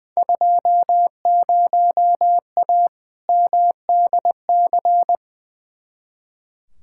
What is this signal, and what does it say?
Morse code, heard as a keyed tone, about 20 wpm, 700 Hz. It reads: 20A MDC
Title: Field Day Exchanges Encore - 20wpm